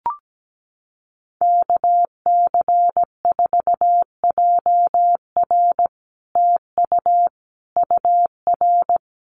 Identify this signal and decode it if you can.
Morse code, heard as a keyed tone, about 17 wpm, 700 Hz. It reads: KC4JR TU UR
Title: Common QSO Phrases Encore 17wpm